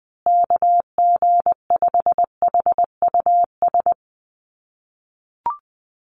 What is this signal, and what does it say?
Morse code, heard as a keyed tone, about 20 wpm, 700 Hz. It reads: KG5HUS
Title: USA Call Signs Repeated 3x 20wpm